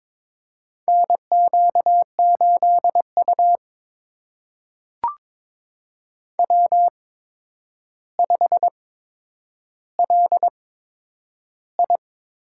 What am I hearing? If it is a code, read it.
Morse code, heard as a keyed tone, about 22 wpm, spaced as for 9 wpm, 700 Hz. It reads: NQ8U W5LI
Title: USA Call Signs - 4 Characters - 8x Character Spacing - 22wpm